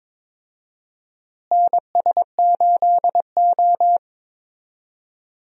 Morse code, heard as a keyed tone, about 22 wpm, 700 Hz. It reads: NS8O
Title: USA Call Signs - 4 Characters - Repeated 2x - 22wpm